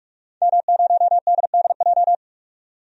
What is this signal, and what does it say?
Morse code, heard as a keyed tone, about 46 wpm, 700 Hz. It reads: M0DDJ
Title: DXCC-100 Call Signs - 5 Characters - 45wpm